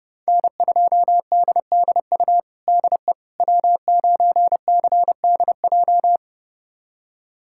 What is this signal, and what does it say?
Morse code, heard as a keyed tone, about 30 wpm, 700 Hz. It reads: N2DDU DE W9CDJ
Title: USA Callsign DE Callsign - 5 Characters - 30wpm